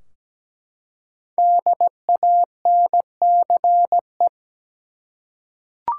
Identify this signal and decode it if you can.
Morse code, heard as a keyed tone, about 17 wpm, 700 Hz. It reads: DANCE
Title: Five Letter Words - 17wpm